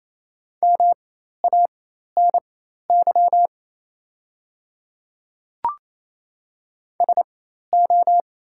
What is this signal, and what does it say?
Morse code, heard as a keyed tone, about 27 wpm, spaced as for 13 wpm, 700 Hz. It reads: MANY SO
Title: 4x Character Spacing - Top 100 Words - 28wpm